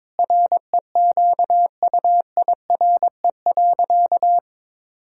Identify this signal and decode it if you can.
Morse code, heard as a keyed tone, about 21 wpm, 700 Hz. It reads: REQUIRE.
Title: Sentences from Top 1000 Words - Review Part 6 - 22wpm